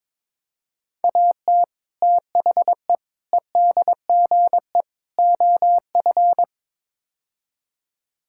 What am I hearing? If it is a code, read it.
Morse code, heard as a keyed tone, about 22 wpm, 700 Hz. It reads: AT THE EDGE OF
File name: practice / 22wpm / Build A Phrase - 5-Word Phrases - 22wpm